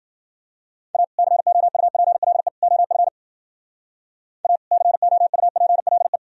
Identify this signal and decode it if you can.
Morse code, heard as a keyed tone, about 49 wpm, 700 Hz. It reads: A COUPLE OF A COUPLE
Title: Speed-Racing - Sets of 3 Words 40wpm